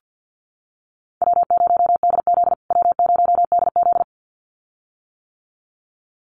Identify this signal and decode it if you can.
Morse code, heard as a keyed tone, about 50 wpm, 700 Hz. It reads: W0DZ W0DZ
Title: USA Call Signs Repeated 2x 50wpm